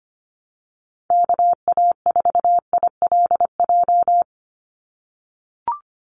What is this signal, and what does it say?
Morse code, heard as a keyed tone, about 25 wpm, 700 Hz. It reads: KA4ILJ
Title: USA Call Signs 25wpm